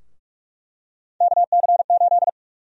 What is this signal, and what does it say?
Morse code, heard as a keyed tone, about 44 wpm, 700 Hz. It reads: KC8
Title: USA Call Sign Prefixes - 45wpm